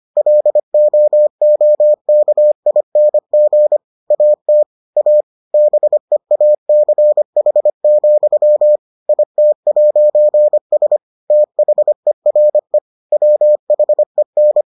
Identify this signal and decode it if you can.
Morse code, heard as a keyed tone, about 25 wpm, 600 Hz. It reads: LOOKING AT A BEACH, IT'S THERE WHEN